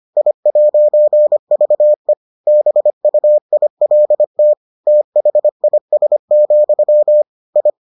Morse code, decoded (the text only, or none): I'VE BUILT THIS, I